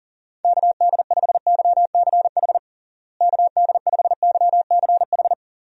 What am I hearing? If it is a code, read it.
Morse code, heard as a keyed tone, about 37 wpm, 700 Hz. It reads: KD5YCH KD5YCH
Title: USA Call Signs - 6 Characters - Repeated 3x - 40wpm